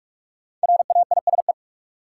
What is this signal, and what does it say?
Morse code, heard as a keyed tone, about 46 wpm, 700 Hz. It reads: RAISE
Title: Sentences from Top 1400 Words - New Word Drill Alt - 45wpm